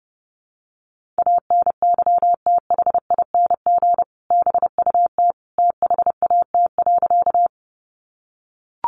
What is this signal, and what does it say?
Morse code, heard as a keyed tone, about 30 wpm, 700 Hz. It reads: ANYTHING BUT THAT.
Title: Sentences from Top 1000 Words 30wpm